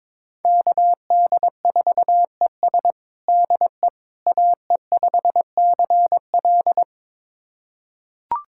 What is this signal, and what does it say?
Morse code, heard as a keyed tone, about 22 wpm, 700 Hz. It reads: KD4ES DE AE5CL
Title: USA Callsign DE Callsign - 5 Characters - 22wpm